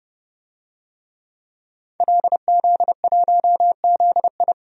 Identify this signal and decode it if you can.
Morse code, heard as a keyed tone, about 31 wpm, 700 Hz. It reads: LZ1ZI
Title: DXCC-100 Call Signs - 5 Characters - 30wpm